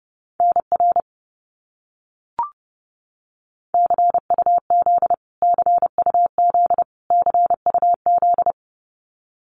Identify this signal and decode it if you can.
Morse code, heard as a keyed tone, about 30 wpm, 700 Hz. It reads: NR CUZ CUZ CUZ
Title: Abbreviations Repeated 3x - 30wpm